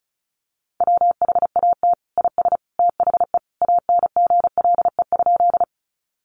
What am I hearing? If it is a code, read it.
Morse code, heard as a keyed tone, about 34 wpm, 700 Hz. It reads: WHAT IS THE ANGLE?
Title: Sentences from Top 1500 Words 35wpm